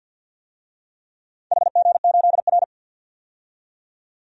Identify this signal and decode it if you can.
Morse code, heard as a keyed tone, about 51 wpm, 700 Hz. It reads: HG8R
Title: DXCC-100 Call Signs - 4 Characters - 50wpm